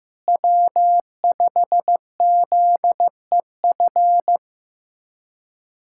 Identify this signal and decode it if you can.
Morse code, heard as a keyed tone, about 16 wpm, 700 Hz. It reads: W5ZEF
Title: USA Callsign DE Callsign - 5 Characters - 15wpm